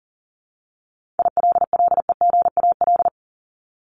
Morse code, heard as a keyed tone, about 40 wpm, 700 Hz. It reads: ILLEGAL
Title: Top 1751-2000 Words 40wpm